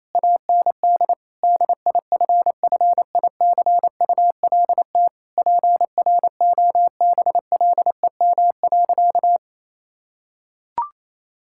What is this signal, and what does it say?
Morse code, heard as a keyed tone, about 28 wpm, 700 Hz. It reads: AND DIFFICULT PROBLEM.
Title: Sentences from Top 1000 Words - Review Part 6 - 28wpm